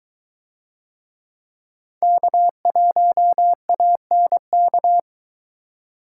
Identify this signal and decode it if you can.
Morse code, heard as a keyed tone, about 22 wpm, 700 Hz. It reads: K1ANK